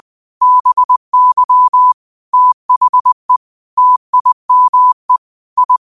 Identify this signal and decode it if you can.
Morse code, heard as a keyed tone, about 20 wpm, 1000 Hz. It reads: BY THE TIME I
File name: qotdfiles / 20wpm